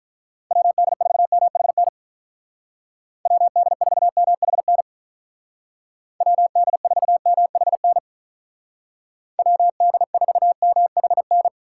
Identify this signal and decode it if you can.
Morse code, heard as a keyed tone, about 44 wpm, 700 Hz. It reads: WD4MHN WD4MHN WD4MHN WD4MHN